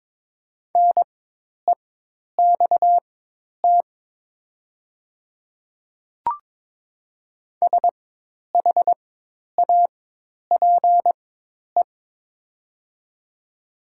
Morse code, heard as a keyed tone, about 22 wpm, spaced as for 10 wpm, 700 Hz. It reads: NEXT SHAPE
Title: 4x Character Spacing - Top 500 Words - 22wpm